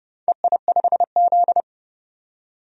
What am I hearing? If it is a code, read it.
Morse code, heard as a keyed tone, about 30 wpm, 700 Hz. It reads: EI5Z
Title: DXCC-100 Call Signs - 4 Characters - 30wpm